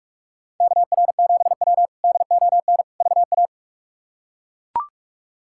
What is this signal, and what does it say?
Morse code, heard as a keyed tone, about 41 wpm, 700 Hz. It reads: KR7W DON VA
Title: K1USN SST QSO Practice - 45wpm